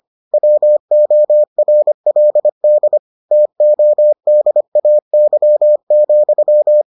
Morse code, heard as a keyed tone, about 25 wpm, 600 Hz. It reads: WORLD TODAY,